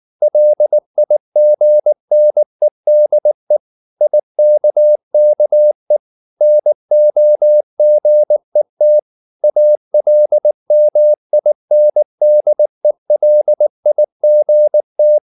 Morse code, heard as a keyed tone, about 19 wpm, 600 Hz. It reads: LIGNEDE IKKE NOGET ALMINDELIGT
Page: Ferie 19 wpm